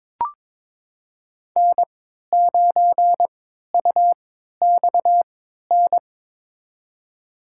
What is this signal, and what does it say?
Morse code, heard as a keyed tone, about 22 wpm, spaced as for 12 wpm, 700 Hz. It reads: N9UXN
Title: USA Call Signs - 3x Character Spacing - 22wpm